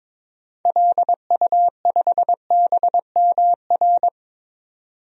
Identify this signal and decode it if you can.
Morse code, heard as a keyed tone, about 21 wpm, 700 Hz. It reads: LU5BMR
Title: DXCC-100 Call Signs - 6 Characters - 22wpm